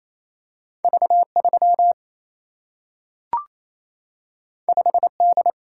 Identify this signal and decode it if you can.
Morse code, heard as a keyed tone, about 28 wpm, 700 Hz. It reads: V 3 5 D
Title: Single-Letter Single-Number 28wpm